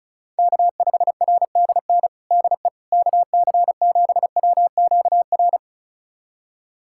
Morse code, heard as a keyed tone, about 36 wpm, 700 Hz. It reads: K5RDN DE KC7WQR
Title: USA Callsign DE Callsign 35wpm